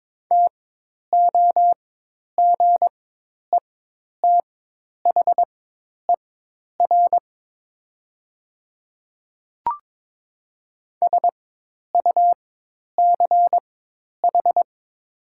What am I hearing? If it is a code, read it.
Morse code, heard as a keyed tone, about 22 wpm, spaced as for 10 wpm, 700 Hz. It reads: TOGETHER SUCH
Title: 4x Character Spacing - Top 300 Words - 22wpm